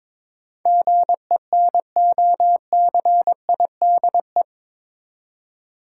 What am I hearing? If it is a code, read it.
Morse code, heard as a keyed tone, about 22 wpm, 700 Hz. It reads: GENOCIDE
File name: practice / 22wpm / Eight Letter Words - 22wpm